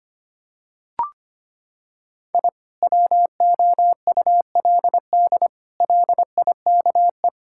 Morse code, heard as a keyed tone, about 25 wpm, 700 Hz. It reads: I WOULD LIKE
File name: practice / 25wpm / Sentences with 4-Word Phrases - 25wpm